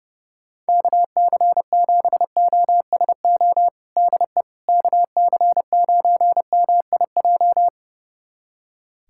Morse code, heard as a keyed tone, about 32 wpm, 700 Hz. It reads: KC7OSO DE KC9MIJ
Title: USA Callsign DE Callsign - 6 Characters - 30wpm